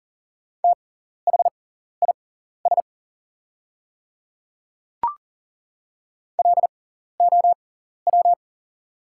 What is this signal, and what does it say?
Morse code, heard as a keyed tone, about 40 wpm, spaced as for 14 wpm, 690 Hz. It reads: THIS LOW
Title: 6x Character Spacing - Top 200 Words - 40wpm